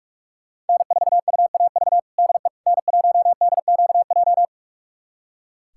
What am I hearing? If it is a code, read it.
Morse code, heard as a keyed tone, about 46 wpm, 700 Hz. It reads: N4UAV DE N1DQJ
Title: USA Callsign DE Callsign - 5 Characters - 45wpm